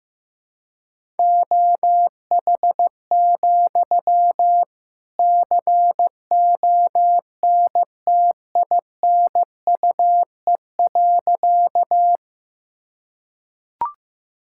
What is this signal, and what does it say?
Morse code, heard as a keyed tone, about 16 wpm, 700 Hz. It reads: OH, CONTINUE.
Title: Sentences from Top 1000 Words - Review Part 6 - 15wpm